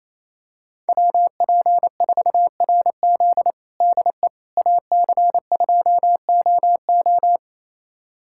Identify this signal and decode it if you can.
Morse code, heard as a keyed tone, about 27 wpm, 700 Hz. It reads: WP4RZ DE AC2OO
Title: USA Callsign DE Callsign - 5 Characters - 28wpm